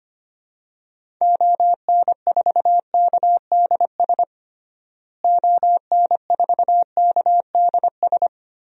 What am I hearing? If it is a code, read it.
Morse code, heard as a keyed tone, about 25 wpm, 700 Hz. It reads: ON4KDS ON4KDS
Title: DXCC-100 Call Signs - 6 Characters - Repeated 3x - 25wpm